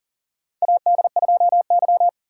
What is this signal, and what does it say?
Morse code, heard as a keyed tone, about 40 wpm, 700 Hz. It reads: AD2Y
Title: USA Call Signs - 4 Characters - 6x Character Spacing - 40wpm